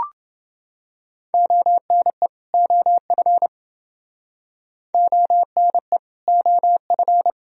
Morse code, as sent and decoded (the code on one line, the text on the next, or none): --- -. / --- ..-. / --- -. / --- ..-.
ONE OF ONE OF